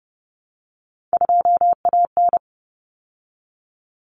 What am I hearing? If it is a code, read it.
Morse code, heard as a keyed tone, about 30 wpm, 700 Hz. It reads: 2AN
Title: Common Cut Numbers - 3-Digit 30wpm